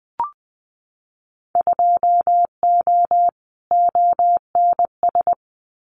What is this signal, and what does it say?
Morse code, heard as a keyed tone, about 20 wpm, 700 Hz. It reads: 2O ONS
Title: Winter Field Day Exchanges Encore 20wpm